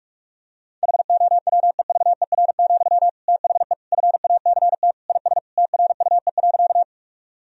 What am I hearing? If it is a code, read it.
Morse code, heard as a keyed tone, about 46 wpm, 700 Hz. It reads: HOWEVER, THE FACT IS TRUE.
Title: Sentences from Top 1000 Words - Review Part 3 - 45wpm